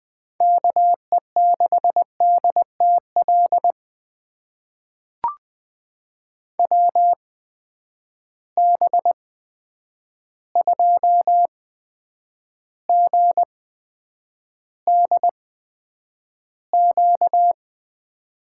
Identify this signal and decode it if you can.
Morse code, heard as a keyed tone, about 20 wpm, spaced as for 9 wpm, 700 Hz. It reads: KE6DTL WB2GDQ
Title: USA Call Signs - 6 Characters - 8x Character Spacing - 20wpm